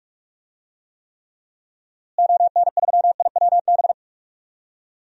Morse code, heard as a keyed tone, about 41 wpm, 700 Hz. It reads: ON3IWB
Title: DXCC-100 Call Signs - 6 Characters - Repeated 3x - 45wpm